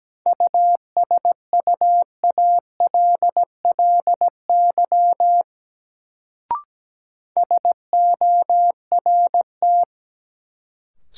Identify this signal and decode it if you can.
Morse code, heard as a keyed tone, about 17 wpm, 700 Hz. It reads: USUALLY SORT
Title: Sentences from Top 1000 Words - Review Part 3 - New Word Drill - 17wpm